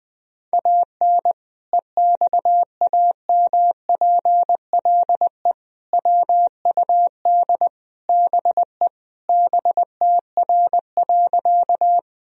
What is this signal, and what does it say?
Morse code, heard as a keyed tone, about 20 wpm, 700 Hz. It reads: AN EXAMPLE WUD BE BTR.
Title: Sentences from Top 400 Words and Abbreviations 20wpm